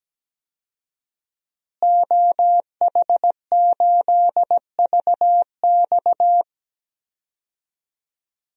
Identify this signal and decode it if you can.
Morse code, heard as a keyed tone, about 17 wpm, 700 Hz. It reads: OH8VX
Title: DXCC-100 Call Signs - 5 Characters - Repeated 2x - 17wpm